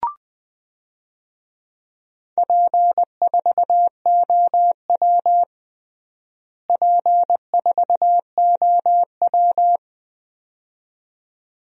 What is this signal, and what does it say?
Morse code, heard as a keyed tone, about 20 wpm, 700 Hz. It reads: P4OW P4OW
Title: DXCC-100 Call Signs - 4 Characters - Repeated 2x - 20wpm